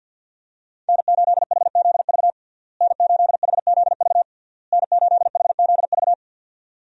Morse code, heard as a keyed tone, about 52 wpm, 700 Hz. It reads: N8HZV N8HZV N8HZV
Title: USA Call Signs - 5 Characters - Repeated 3x - 50wpm